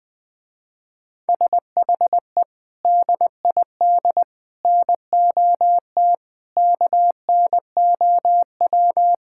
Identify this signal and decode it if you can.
Morse code, heard as a keyed tone, about 20 wpm, 700 Hz. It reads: SHE DID NOT KNOW
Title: Sentences with 3-Word Phrases - Spoken Beginning - 20wpm